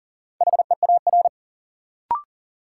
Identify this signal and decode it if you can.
Morse code, heard as a keyed tone, about 40 wpm, 700 Hz. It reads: HEAR
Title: Sentences from Top 300 Words - New Word Drill - 40wpm